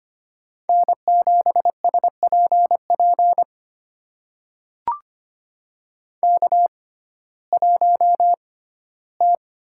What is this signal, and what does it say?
Morse code, heard as a keyed tone, about 25 wpm, spaced as for 15 wpm, 700 Hz. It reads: N7SPP K1T
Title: USA Call Signs - 5 Characters - 6x Character Spacing - 25wpm